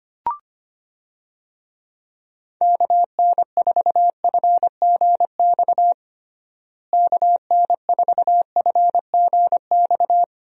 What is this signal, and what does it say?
Morse code, heard as a keyed tone, about 25 wpm, 700 Hz. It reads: KN4FGX KN4FGX
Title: USA Call Signs - 6 Characters - Repeated 2x - 25wpm